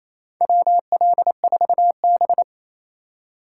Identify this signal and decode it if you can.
Morse code, heard as a keyed tone, about 28 wpm, 700 Hz. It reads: WL4B
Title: USA Call Signs - 4 Characters - 3x Character Spacing - 28wpm